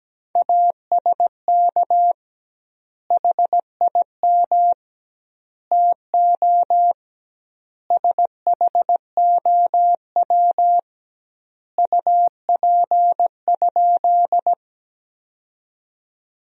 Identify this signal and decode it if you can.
Morse code, heard as a keyed tone, about 17 wpm, 700 Hz. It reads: ASK HIM TO SHOW UP?
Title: 2x Word-Spacing - Sentences from Top 200 Words - 17wpm